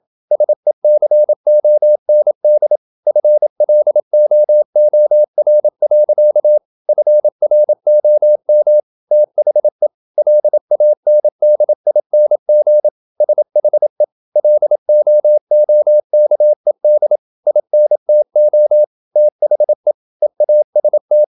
Morse code, heard as a keyed tone, about 27 wpm, 600 Hz. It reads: SECOND FLOOR. FROM THE LANDING SHE LOOKED INTO THE EAST
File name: Never 0027wpm